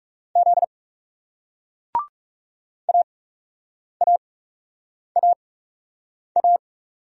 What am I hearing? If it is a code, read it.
Morse code, heard as a keyed tone, about 40 wpm, spaced as for 8 wpm, 700 Hz. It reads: Z AAAA